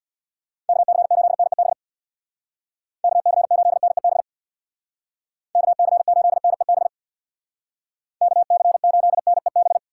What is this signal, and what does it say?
Morse code, heard as a keyed tone, about 59 wpm, 700 Hz. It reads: KC8NEB KC8NEB KC8NEB KC8NEB
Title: Speed Racing - USA 2x3 Call Signs 50wpm